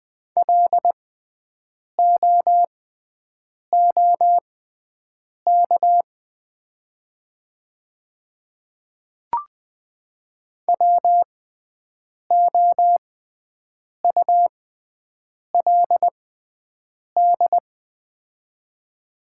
Morse code, heard as a keyed tone, about 20 wpm, spaced as for 7 wpm, 700 Hz. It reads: LOOK WOULD
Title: 6x Character Spacing - Top 300 Words - 20wpm